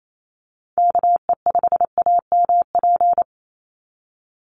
Morse code, heard as a keyed tone, about 28 wpm, 700 Hz. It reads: KE5AMP
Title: USA Call Signs 28wpm